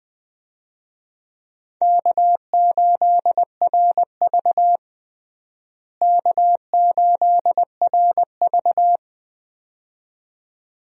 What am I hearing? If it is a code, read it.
Morse code, heard as a keyed tone, about 20 wpm, 700 Hz. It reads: K8RV K8RV
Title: USA Call Signs - 4 Characters - Repeated 2x - 20wpm